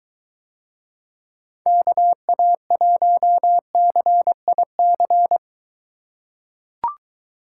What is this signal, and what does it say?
Morse code, heard as a keyed tone, about 21 wpm, 700 Hz. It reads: KA1CIC